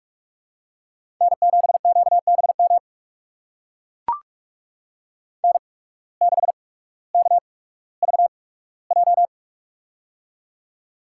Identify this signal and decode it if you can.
Morse code, heard as a keyed tone, about 44 wpm, spaced as for 19 wpm, 700 Hz. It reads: N7QBM N6KVJ
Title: USA Call Signs - 5 Characters - 8x Character Spacing - 45wpm